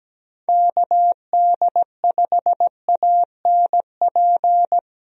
Morse code, heard as a keyed tone, about 17 wpm, 700 Hz. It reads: KD5ANP
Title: USA Call Signs - 6 Characters - 17wpm